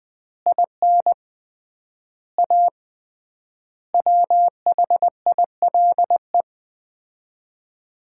Morse code, heard as a keyed tone, about 20 wpm, 700 Hz. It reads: IN A WHILE
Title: 3x Word-Spacing - Sets of 5 Words - 20wpm